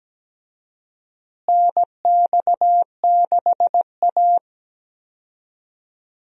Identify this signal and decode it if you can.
Morse code, heard as a keyed tone, about 17 wpm, 700 Hz. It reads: NX6A
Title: USA Call Signs - 4 Characters - Repeated 3x - 17wpm